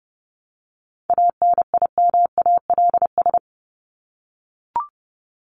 Morse code, heard as a keyed tone, about 30 wpm, 700 Hz. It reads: ANIMALS
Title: ES:S Suffix - Top 100 Words with Base Words 30wpm